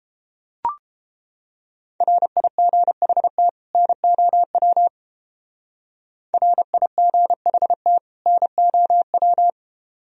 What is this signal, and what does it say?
Morse code, heard as a keyed tone, about 30 wpm, 700 Hz. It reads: RIGHT NOW RIGHT NOW